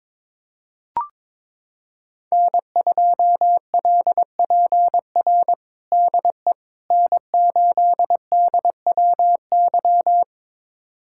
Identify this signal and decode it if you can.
Morse code, heard as a keyed tone, about 22 wpm, 700 Hz. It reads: N2LPR DE N8DWY
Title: USA Callsign DE Callsign - 5 Characters - 22wpm